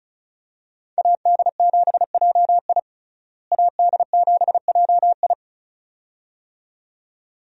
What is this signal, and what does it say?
Morse code, heard as a keyed tone, about 34 wpm, 700 Hz. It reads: AD7JI AD7JI
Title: USA Call Signs - 5 Characters - Repeated 2x - 35wpm